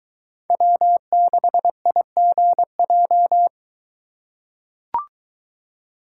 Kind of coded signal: Morse code

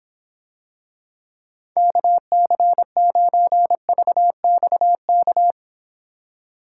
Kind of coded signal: Morse code

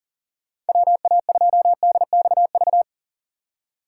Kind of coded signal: Morse code

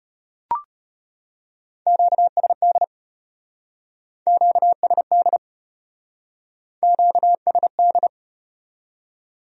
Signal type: Morse code

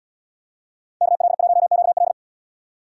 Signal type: Morse code

